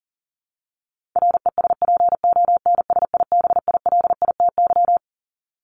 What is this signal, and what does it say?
Morse code, heard as a keyed tone, about 40 wpm, 700 Hz. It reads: RESPONSIBILITY
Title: Top 1001-1250 Words 40wpm